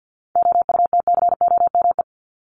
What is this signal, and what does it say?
Morse code, heard as a keyed tone, about 48 wpm, 700 Hz. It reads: OUTCOME
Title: Top 1501-1750 Words 50wpm